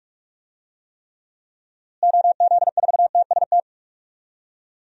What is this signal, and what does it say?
Morse code, heard as a keyed tone, about 43 wpm, 700 Hz. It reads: OZ4TST
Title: DXCC-100 Call Signs - 6 Characters - Repeated 3x - 45wpm